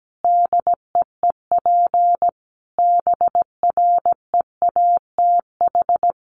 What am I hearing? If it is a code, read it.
Morse code, heard as a keyed tone, about 17 wpm, 700 Hz. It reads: DEEP BREATH
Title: Sets of 5 Words Encore 17wpm